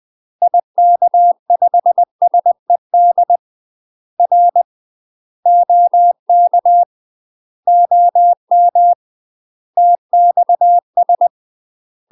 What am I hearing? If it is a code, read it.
Morse code, heard as a keyed tone, about 20 wpm, 700 Hz. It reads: IK5SED R OK OM TXS
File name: UK MASTER QSOs TRAINING / 20WPM / UK MASTER QSOs 0015